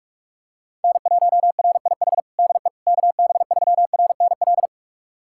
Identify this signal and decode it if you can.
Morse code, heard as a keyed tone, about 46 wpm, 700 Hz. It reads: N1RIH DE KB3RNL
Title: USA Callsign DE Callsign 45wpm